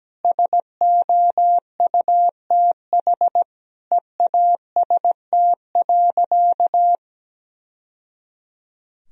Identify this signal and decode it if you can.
Morse code, heard as a keyed tone, about 17 wpm, 700 Hz. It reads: SOUTH EAST.
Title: Sentences from Top 1000 Words - Review Part 3 - 17wpm